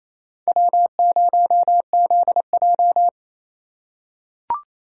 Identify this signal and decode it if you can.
Morse code, heard as a keyed tone, about 28 wpm, 700 Hz. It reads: W0ZJ
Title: USA Call Signs - 4 Characters - 2x Character Spacing - 28wpm